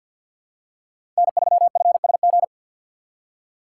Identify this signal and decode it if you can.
Morse code, heard as a keyed tone, about 50 wpm, 700 Hz. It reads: N3FSG
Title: USA Call Signs - 5 Characters - 6x Character Spacing - 50wpm